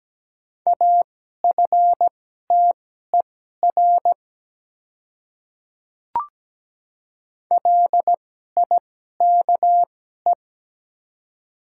Morse code, heard as a keyed tone, about 17 wpm, spaced as for 12 wpm, 700 Hz. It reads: AFTER LIKE
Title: 2x Character Spacing - Top 300 Words - 17wpm